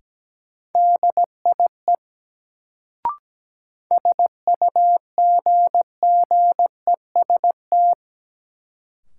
Morse code, heard as a keyed tone, about 17 wpm, 700 Hz. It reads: DIE SUGGEST
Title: Sentences from Top 1300 Words - New Word Drill - 17wpm